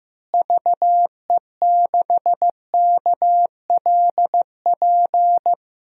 Morse code, heard as a keyed tone, about 15 wpm, 700 Hz. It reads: VE6KLP
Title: DXCC-100 Call Signs - 6 Characters - Repeated 2x - 15wpm